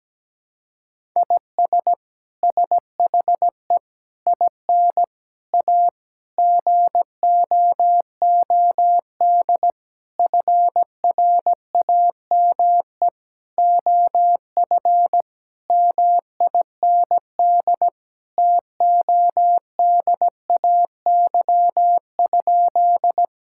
Morse code, heard as a keyed tone, about 17 wpm, 700 Hz. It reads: IS SHE IN A GOOD FRAME OF MIND TODAY?
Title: Sentences from Top 1000 Words - Review Part 6 - 17wpm